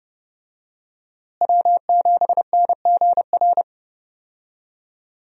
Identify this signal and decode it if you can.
Morse code, heard as a keyed tone, about 30 wpm, 700 Hz. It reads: W7NGR
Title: USA Call Signs - 5 Characters - Repeated 2x - 30wpm